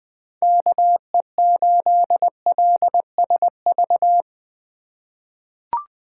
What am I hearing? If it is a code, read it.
Morse code, heard as a keyed tone, about 20 wpm, 700 Hz. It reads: KE8LSV
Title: USA Call Signs - 3x Character Spacing - 20wpm